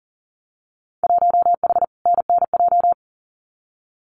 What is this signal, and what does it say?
Morse code, heard as a keyed tone, about 40 wpm, 700 Hz. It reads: 1H NNJ
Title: Winter Field Day Exchanges 40wpm